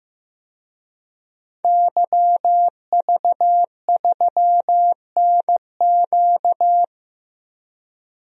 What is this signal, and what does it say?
Morse code, heard as a keyed tone, about 15 wpm, 700 Hz. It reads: YV3NQ
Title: DXCC-100 Call Signs - 5 Characters - 15wpm